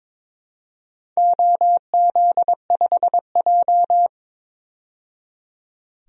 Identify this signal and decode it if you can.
Morse code, heard as a keyed tone, about 22 wpm, 700 Hz. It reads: OZ5J
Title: DXCC-100 Call Signs - 4 Characters - 22wpm